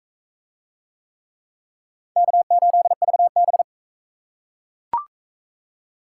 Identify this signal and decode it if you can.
Morse code, heard as a keyed tone, about 39 wpm, 700 Hz. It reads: K8VB